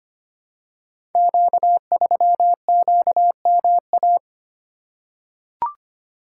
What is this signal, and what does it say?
Morse code, heard as a keyed tone, about 25 wpm, 700 Hz. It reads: Q3QMA
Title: Code Groups - 5 Letter:Numbers - 25wpm